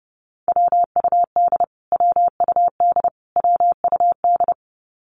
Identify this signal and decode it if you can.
Morse code, heard as a keyed tone, about 30 wpm, 700 Hz. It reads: WUD WUD WUD
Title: Abbreviations Repeated 3x - 30wpm